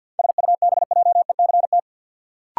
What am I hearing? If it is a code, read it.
Morse code, heard as a keyed tone, about 53 wpm, 700 Hz. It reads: SUBJECT
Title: Sentences from Top 800 Words - New Word Drill - 50wpm